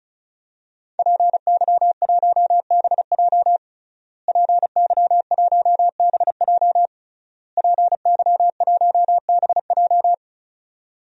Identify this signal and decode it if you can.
Morse code, heard as a keyed tone, about 36 wpm, 700 Hz. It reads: PY1BJ PY1BJ PY1BJ
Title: DXCC-100 Call Signs - 5 Characters - Repeated 3x - 35wpm